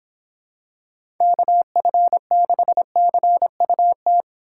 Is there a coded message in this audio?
Yes: Morse code from 1.2 to 4.2 s, heard as a keyed tone, about 26 wpm, 700 Hz. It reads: KF6CUT